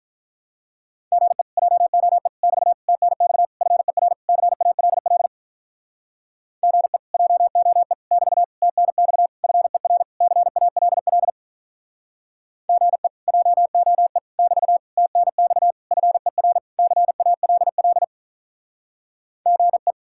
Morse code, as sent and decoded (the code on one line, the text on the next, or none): --. / .--- --- . / -...- / - -. -..- / ..-. .-. / -.-. .- .-.. .-.. / --. / .--- --- . / -...- / - -. -..- / ..-. .-. / -.-. .- .-.. .-.. / --. / .--- --- . / -...- / - -. -..- / ..-. .-. / -.-. .- .-.. .-.. / --. .
GE JOE <BT> TNX FER CALL GE JOE <BT> TNX FER CALL GE JOE <BT> TNX FER CALL GE